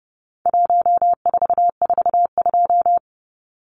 Morse code, heard as a keyed tone, about 30 wpm, 700 Hz. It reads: 1442
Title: Numbers - 4-Digit 30wpm